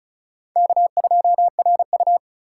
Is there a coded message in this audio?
Yes: Morse code from 0.6 to 2.2 s, heard as a keyed tone, about 44 wpm, 700 Hz. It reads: K2RU